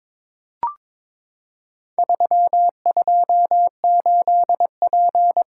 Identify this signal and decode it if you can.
Morse code, heard as a keyed tone, about 22 wpm, 700 Hz. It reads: 328P
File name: practice / 22wpm / Four Letter-Number Combinations - 22wpm